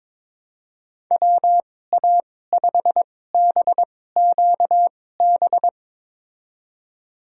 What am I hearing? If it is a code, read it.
Morse code, heard as a keyed tone, about 22 wpm, spaced as for 16 wpm, 700 Hz. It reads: WA5BQB
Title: USA Call Signs - 6 Characters - 2x Character Spacing - 22wpm